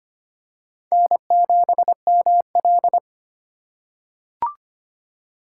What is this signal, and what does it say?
Morse code, heard as a keyed tone, about 25 wpm, 700 Hz. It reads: N7ML
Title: K1USN SST QSO Practice - 25wpm